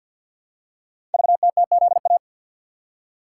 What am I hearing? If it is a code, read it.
Morse code, heard as a keyed tone, about 48 wpm, 700 Hz. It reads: VTTZA
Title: Code Groups - 5 Letters - 50wpm